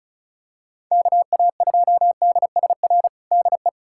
Morse code, heard as a keyed tone, about 36 wpm, 700 Hz. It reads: KA2DSR DE
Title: USA Callsign DE Callsign 35wpm